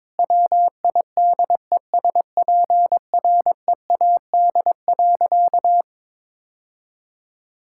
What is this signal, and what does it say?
Morse code, heard as a keyed tone, about 22 wpm, 700 Hz. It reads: WIDESPREAD.
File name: practice / 22wpm / Sentences with 3-Word Phrases - 22wpm